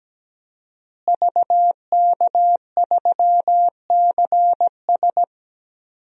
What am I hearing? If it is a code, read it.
Morse code, heard as a keyed tone, about 17 wpm, 700 Hz. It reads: VK3CS
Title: DXCC-100 Call Signs - 5 Characters - Repeated 2x - 17wpm